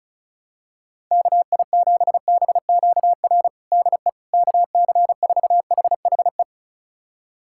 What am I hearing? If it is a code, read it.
Morse code, heard as a keyed tone, about 36 wpm, 690 Hz. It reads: KI7BQR DE KC4HHE
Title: USA Callsign DE Callsign 35wpm